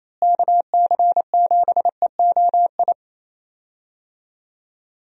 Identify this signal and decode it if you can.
Morse code, heard as a keyed tone, about 28 wpm, 700 Hz. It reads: KC7EOI
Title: USA Call Signs - 6 Characters - Repeated 2x - 28wpm